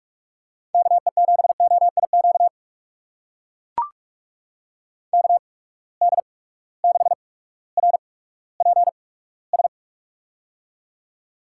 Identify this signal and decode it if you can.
Morse code, heard as a keyed tone, about 41 wpm, spaced as for 19 wpm, 700 Hz. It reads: KE7OIQ KD6RPS
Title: USA Call Signs - 6 Characters - 8x Character Spacing - 45wpm